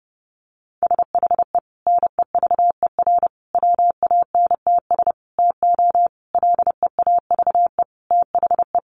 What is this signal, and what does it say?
Morse code, heard as a keyed tone, about 30 wpm, 700 Hz. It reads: SHE NEVER WANTS TO LEAVE THE
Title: Easy Phrases 30wpm